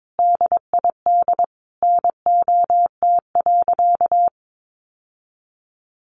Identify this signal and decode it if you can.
Morse code, heard as a keyed tone, about 22 wpm, 700 Hz. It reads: DID NOT.
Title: Sentences from Top 700 Words 22wpm